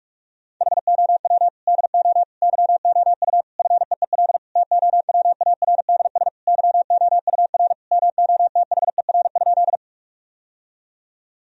Morse code, heard as a keyed tone, about 41 wpm, 700 Hz. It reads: HOW DO YOU FEEL TOWARDS YOUR MOTHER?
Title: Sentences from Top 1000 Words - Review Part 2 - 45wpm